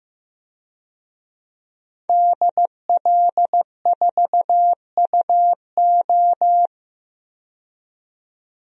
Morse code, heard as a keyed tone, about 15 wpm, 700 Hz. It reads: DL4UO
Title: DXCC-100 Call Signs - 5 Characters - 15wpm